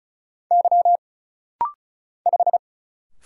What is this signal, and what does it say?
Morse code, heard as a keyed tone, about 34 wpm, 700 Hz. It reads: Y 5
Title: Single Letter-Number Alt - Rapid-Fire 35wpm